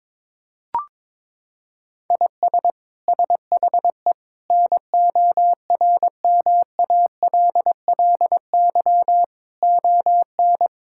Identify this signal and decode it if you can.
Morse code, heard as a keyed tone, about 21 wpm, 700 Hz. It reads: IS SHE NORMALLY ON
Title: Sentences from Top 1000 Words - Review Part 6 - 22wpm